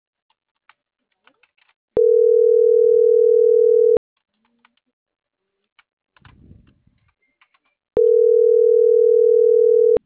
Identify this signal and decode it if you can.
Ringback tone (North America), 440 + 480 Hz.
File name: CENTRAL MAYAB / out dulce.silva